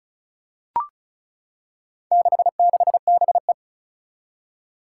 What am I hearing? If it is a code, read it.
Morse code, heard as a keyed tone, about 36 wpm, 700 Hz. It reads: 66BE
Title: Four Challenging Letter-Number Combinations 35wpm